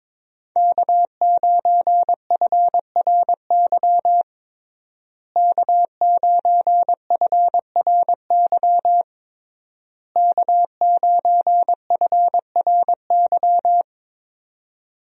Morse code, heard as a keyed tone, about 22 wpm, 700 Hz. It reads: K9FRY K9FRY K9FRY
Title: USA Call Signs - 5 Characters - Repeated 3x - 22wpm